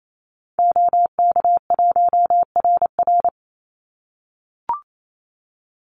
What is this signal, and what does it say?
Morse code, heard as a keyed tone, about 28 wpm, 700 Hz. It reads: OK1RR
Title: Call Signs - 28wpm